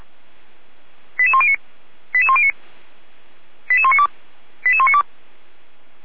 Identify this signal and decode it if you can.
Five-tone selective call (ZVEI), heard as quick runs of five tones, tones of about 70 ms. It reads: 88109 88109 88192 88192
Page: ZVEI